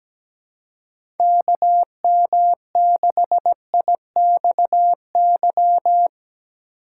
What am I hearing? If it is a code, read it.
Morse code, heard as a keyed tone, about 17 wpm, 700 Hz. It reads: KM6IXY
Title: USA Call Signs - 6 Characters - 2x Character Spacing - 17wpm